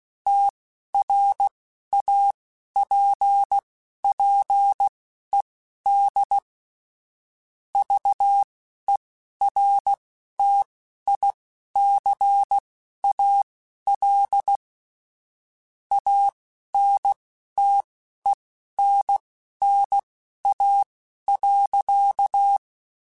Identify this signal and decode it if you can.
Morse code, heard as a keyed tone, about 16 wpm, spaced as for 11 wpm, 800 Hz. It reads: TRAPPED VERTICAL ANTENNA.
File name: QSOs / QSOs-mp3s / kc7haa@10